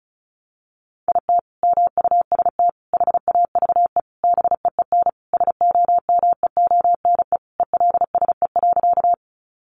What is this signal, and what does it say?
Morse code, heard as a keyed tone, about 34 wpm, 700 Hz. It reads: IT MUST HAVE BEEN SOMEONE ELSE.
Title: Sentences from Top 800 Words 35wpm